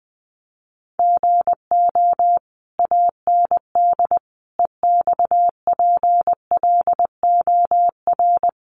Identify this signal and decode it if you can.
Morse code, heard as a keyed tone, about 20 wpm, 700 Hz. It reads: GO AND EXPLOR
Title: Sentences from Top 1100 Words 20wpm